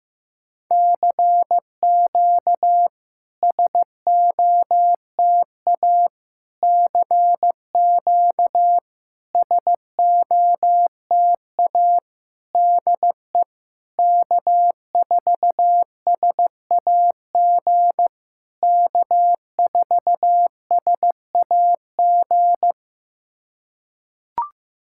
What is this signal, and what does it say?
Morse code, heard as a keyed tone, about 16 wpm, 700 Hz. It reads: CQ SOTA CQ SOTA DE K4SAG K4SAG
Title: SOTA QSO Practice 15wpm